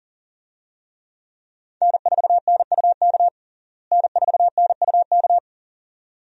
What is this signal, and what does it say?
Morse code, heard as a keyed tone, about 40 wpm, 700 Hz. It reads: N4NUK N4NUK
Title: USA Call Signs - 5 Characters - Repeated 2x - 40wpm